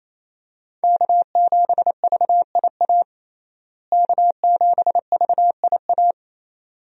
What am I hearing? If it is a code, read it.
Morse code, heard as a keyed tone, about 28 wpm, 700 Hz. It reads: K7VIA K7VIA
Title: USA Call Signs - 5 Characters - Repeated 2x - 28wpm